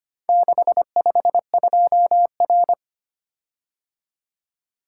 Morse code, heard as a keyed tone, about 25 wpm, 700 Hz. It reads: 652R
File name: practice / 25wpm / Four Letter-Number Combinations - 25wpm